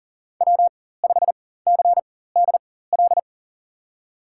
Morse code, heard as a keyed tone, about 40 wpm, spaced as for 19 wpm, 700 Hz. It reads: W5CDL
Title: USA Call Signs - 5 Characters - 4x Character Spacing - 40wpm